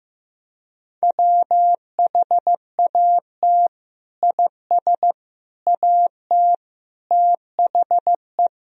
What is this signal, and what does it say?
Morse code, heard as a keyed tone, about 15 wpm, 700 Hz. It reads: WHAT IS AT THE
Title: Sentences from Top 1000 Words - Review Part 2 - 15wpm